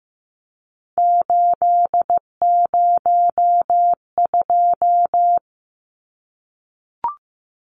Morse code, heard as a keyed tone, about 15 wpm, 700 Hz. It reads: 802
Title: Numbers - 3-Digit 15wpm